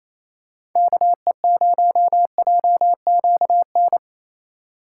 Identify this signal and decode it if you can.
Morse code, heard as a keyed tone, about 28 wpm, 700 Hz. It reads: KE0JQN
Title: USA Call Signs - 6 Characters - 8x Character Spacing - 28wpm